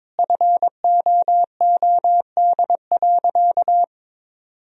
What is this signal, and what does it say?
Morse code, heard as a keyed tone, about 22 wpm, 700 Hz. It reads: FOOD.
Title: Sentences from Top 1000 Words - Review Part 4 - 22wpm